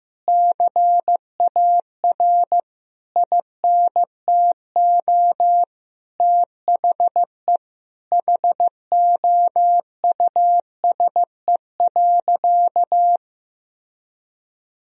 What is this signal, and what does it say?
Morse code, heard as a keyed tone, about 15 wpm, 700 Hz. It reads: CAR INTO THE HOUSE.
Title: Sentences from Top 1000 Words - Review Part 1 - 15wpm